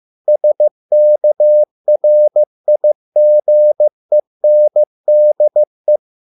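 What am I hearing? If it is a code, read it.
Morse code, heard as a keyed tone, about 15 wpm, 600 Hz. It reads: SKRIGENDE
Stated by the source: Morse code at a slow 15 wpm from CW med Gnister.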